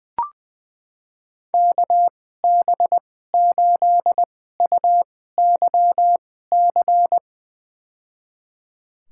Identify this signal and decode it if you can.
Morse code, heard as a keyed tone, about 20 wpm, spaced as for 14 wpm, 700 Hz. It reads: KB8UYC
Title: USA Call Signs - 2x Character Spacing - 20wpm